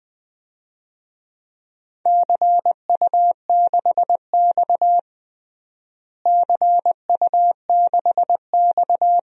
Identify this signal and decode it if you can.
Morse code, heard as a keyed tone, about 21 wpm, 700 Hz. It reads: CU6X CU6X
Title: DXCC-100 Call Signs - 4 Characters - Repeated 2x - 20wpm